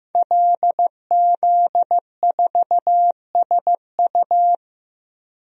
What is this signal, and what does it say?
Morse code, heard as a keyed tone, about 15 wpm, 700 Hz. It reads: LZ4SU
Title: DXCC-100 Call Signs - 5 Characters - 15wpm